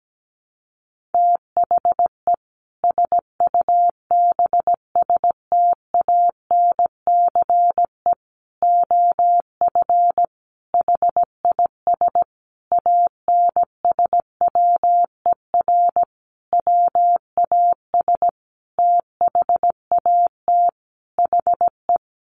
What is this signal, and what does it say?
Morse code, heard as a keyed tone, about 17 wpm, 700 Hz. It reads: THE SUBSTANCE OF HIS ANSWER WAS THAT HE
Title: Sentences from Top 1500 Words 17wpm